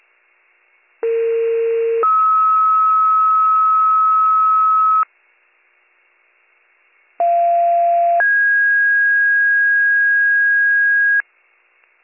Two-tone page, heard as a one-second tone, then a three-second tone, about 460 then 1290 Hz, about 690 then 1650 Hz. It is heard from 1.0 to 11.2 s.